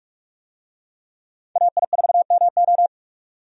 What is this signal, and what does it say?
Morse code, heard as a keyed tone, about 46 wpm, 700 Hz. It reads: AI4MO
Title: USA Call Signs - 5 Characters - 45wpm